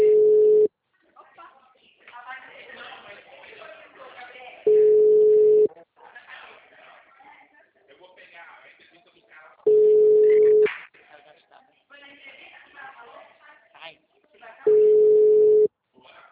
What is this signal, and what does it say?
Ringback tone (Europe, 425 Hz), with speech.